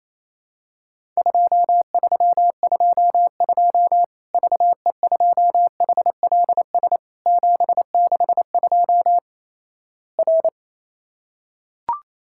Morse code, heard as a keyed tone, about 28 wpm, 700 Hz. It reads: 2322 VE2HLS 762 R
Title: Worked All Europe DX Contest - QTC Exchange - 28wpm